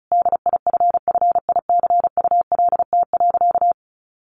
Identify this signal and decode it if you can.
Morse code, heard as a keyed tone, about 34 wpm, 700 Hz. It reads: DIFFICULT.
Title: Sentences from Top 1500 Words 35wpm